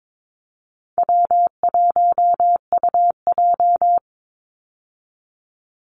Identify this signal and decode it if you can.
Morse code, heard as a keyed tone, about 22 wpm, 700 Hz. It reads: W1UJ
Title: Call Signs - 22wpm